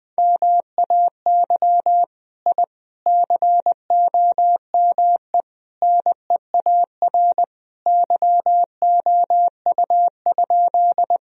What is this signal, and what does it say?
Morse code, heard as a keyed tone, about 21 wpm, 700 Hz. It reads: MAY I COME NEAR YOU?
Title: Sentences from Top 1000 Words - Review Part 1 - 20wpm